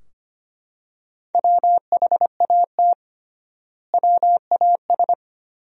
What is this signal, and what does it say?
Morse code, heard as a keyed tone, about 25 wpm, 700 Hz. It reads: WHAT WAS
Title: 3x Word-Spacing - Sentences from Top 100 Words - 25wpm